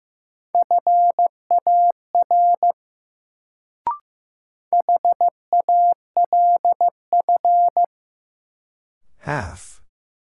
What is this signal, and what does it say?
Morse code, heard as a keyed tone, about 15 wpm, 700 Hz. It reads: FAR HALF
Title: Sets of 3 Words - New Word Drill - 15wpm